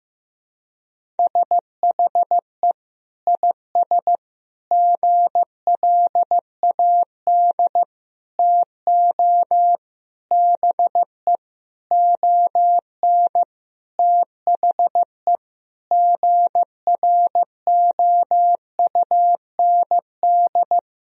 Morse code, heard as a keyed tone, about 15 wpm, 700 Hz. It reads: SHE IS GLAD TO BE ON THE GROUND
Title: Sentences from Top 1000 Words - Review Part 2 - 15wpm